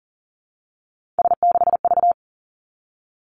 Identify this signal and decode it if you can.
Morse code, heard as a keyed tone, about 43 wpm, 700 Hz. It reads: S6V
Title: Three Challenging Letter-Number Combinations 40wpm